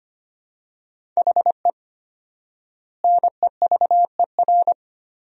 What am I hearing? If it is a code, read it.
Morse code, heard as a keyed tone, about 25 wpm, 700 Hz. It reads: HE NEVER
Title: 4x Word-Spacing - Sentences from Top 500 Words - 25wpm